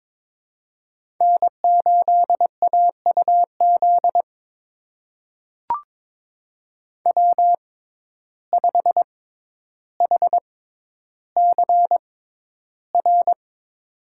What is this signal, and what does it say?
Morse code, heard as a keyed tone, about 22 wpm, spaced as for 11 wpm, 700 Hz. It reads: N8AUZ W5HCR
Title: USA Call Signs - 6x Character Spacing - 22wpm